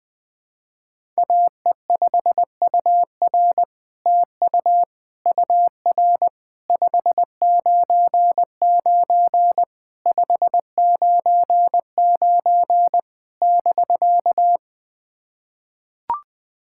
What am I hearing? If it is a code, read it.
Morse code, heard as a keyed tone, about 20 wpm, 700 Hz. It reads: AE5UR TU UR 599 599 <BK>
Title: SOTA QSO Practice 20wpm